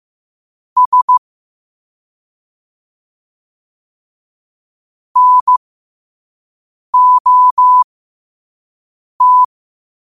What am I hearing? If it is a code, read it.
Morse code, heard as a keyed tone, about 15 wpm, spaced as for 5 wpm, 1000 Hz. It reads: S NOT
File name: qotdfiles / 5wpm